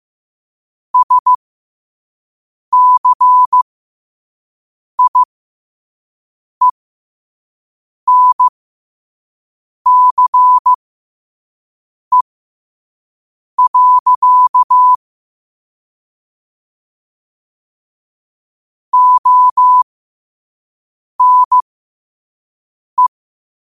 Morse code, heard as a keyed tone, about 15 wpm, spaced as for 5 wpm, 1000 Hz. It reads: SCIENCE. ONE
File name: qotdfiles / 5wpm